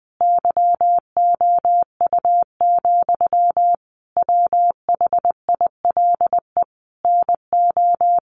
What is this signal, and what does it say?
Morse code, heard as a keyed tone, about 20 wpm, 700 Hz. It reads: YOU, WHILE NO
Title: Sentences from Top 1400 Words 20wpm